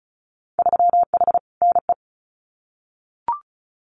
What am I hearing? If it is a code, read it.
Morse code, heard as a keyed tone, about 36 wpm, 700 Hz. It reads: 3H NE
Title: Winter Field Day Exchanges Encore 35wpm